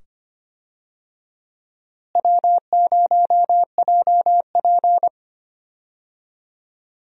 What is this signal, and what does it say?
Morse code, heard as a keyed tone, about 25 wpm, 700 Hz. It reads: W0JP
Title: USA Call Signs - 4 Characters - Repeated 2x - 25wpm